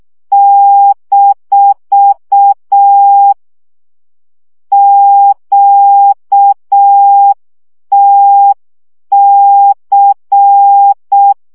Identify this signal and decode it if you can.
Morse code, heard as a keyed tone, about 6 wpm, 800 Hz. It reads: - QTC
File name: morze vizsga 2011-2 5